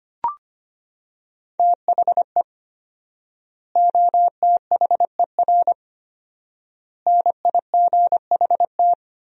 Morse code, heard as a keyed tone, about 25 wpm, 700 Hz. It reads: THE OTHER NIGHT
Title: 4x Word-Spacing - Sentences from Top 300 Words - 25wpm